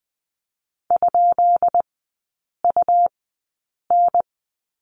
Morse code, heard as a keyed tone, about 20 wpm, spaced as for 8 wpm, 700 Hz. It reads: ?UN
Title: 133 Lesson 22e - Question Mark - Three Characters 20wpm